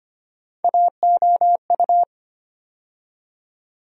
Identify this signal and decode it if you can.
Morse code, heard as a keyed tone, about 25 wpm, 700 Hz. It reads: AOU
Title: USA Call Sign Postfixes - 3 Letters 25wpm